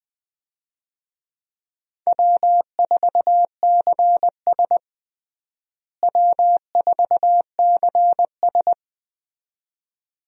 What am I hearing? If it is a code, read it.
Morse code, heard as a keyed tone, about 20 wpm, 700 Hz. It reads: W4CS W4CS
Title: USA Call Signs - 4 Characters - Repeated 2x - 20wpm